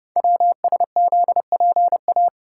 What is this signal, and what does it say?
Morse code, heard as a keyed tone, about 30 wpm, 700 Hz. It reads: WSZPA
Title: Code Groups - 5 Letters - 30wpm